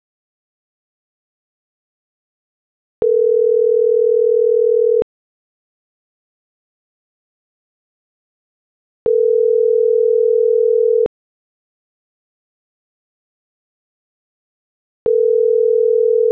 Ringback tone (North America), 440 + 480 Hz.